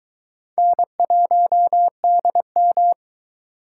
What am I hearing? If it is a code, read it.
Morse code, heard as a keyed tone, about 22 wpm, 700 Hz. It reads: N1DM